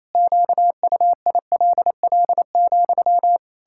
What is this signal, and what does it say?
Morse code, heard as a keyed tone, about 28 wpm, 700 Hz. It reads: QUILL,
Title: Pangram Challenge v2 - 28wpm